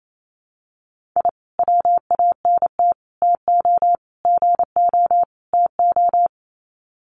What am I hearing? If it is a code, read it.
Morse code, heard as a keyed tone, about 28 wpm, 700 Hz. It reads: I WANT TO GO TO
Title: Sets of 5 Words Encore 28wpm